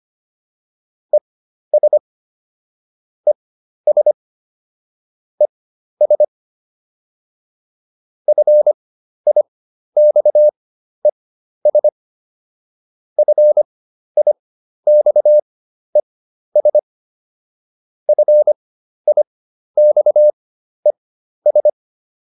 Morse code, heard as a keyed tone, about 25 wpm, spaced as for 12 wpm, 600 Hz. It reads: ES ES ES FIXES FIXES FIXES